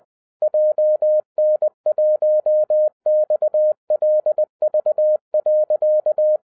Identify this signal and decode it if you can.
Morse code, heard as a keyed tone, about 20 wpm, 610 Hz. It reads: JN1XLV.
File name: 2011-07-mp3 / n58 BACK FROM A BIG ACCIDENT 20wpm